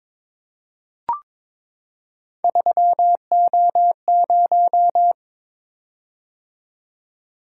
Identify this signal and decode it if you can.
Morse code, heard as a keyed tone, about 22 wpm, 700 Hz. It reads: 3O0
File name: practice / 22wpm / Three Letter-Number Combinations - 22wpm